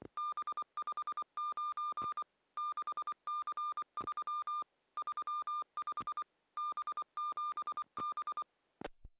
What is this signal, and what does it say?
Morse code, heard as a keyed tone, about 24 wpm, 1200 Hz. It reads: B59 6C3 35 B7B